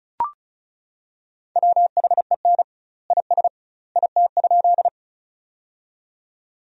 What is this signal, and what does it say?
Morse code, heard as a keyed tone, about 34 wpm, 700 Hz. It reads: WHEN IS IT?
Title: 2x Word-Spacing - Sentences from Top 100 Words - 35wpm